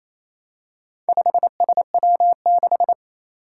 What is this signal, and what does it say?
Morse code, heard as a keyed tone, about 27 wpm, 700 Hz. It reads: 5SW6
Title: Four Challenging Letter-Number Combinations 28wpm